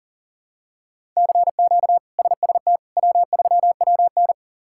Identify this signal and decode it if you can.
Morse code, heard as a keyed tone, about 40 wpm, 700 Hz. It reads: CQ SST W3WN
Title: K1USN SST QSO Practice - 40wpm